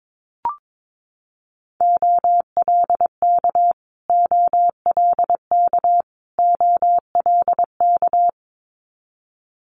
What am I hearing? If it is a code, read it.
Morse code, heard as a keyed tone, about 22 wpm, 700 Hz. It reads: OLK OLK OLK
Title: Multi-letter Phonemes Repeated 3x 22wpm